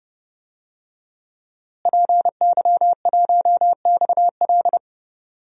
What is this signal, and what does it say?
Morse code, heard as a keyed tone, about 30 wpm, 700 Hz. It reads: PY1XL
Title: DXCC-100 Call Signs - 5 Characters - Repeated 2x - 30wpm